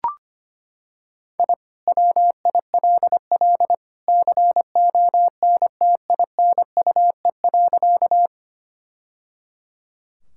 Morse code, heard as a keyed tone, about 25 wpm, 700 Hz. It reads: I WILL CONTINUE.
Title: Sentences from Top 1000 Words - Review Part 6 - 25wpm